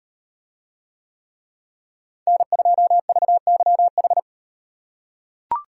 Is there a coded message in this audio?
Yes: Morse code from 2.3 to 4.2 s, heard as a keyed tone, about 38 wpm, 700 Hz. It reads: N2VYH